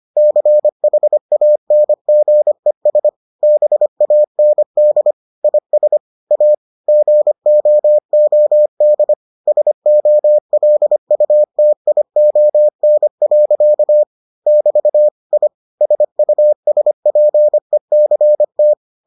Morse code, heard as a keyed tone, about 25 wpm, 600 Hz. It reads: CHANGES BAND IS A GOOD SOLUTION. <BT> I SUSPECT